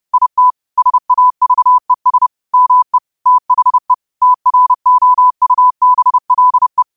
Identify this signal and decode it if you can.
Morse code, heard as a keyed tone, about 30 wpm, 1000 Hz. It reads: IT SAVES ME THE TROUBLE
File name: qotdfiles / 30wpm